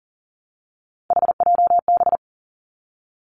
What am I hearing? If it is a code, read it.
Morse code, heard as a keyed tone, about 42 wpm, 690 Hz. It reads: HJB
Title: Three Challenging Letter-Number Combinations 40wpm